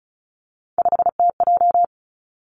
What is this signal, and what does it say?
Morse code, heard as a keyed tone, about 36 wpm, 700 Hz. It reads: 5TJ
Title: Three Challenging Letter-Number Combinations 35wpm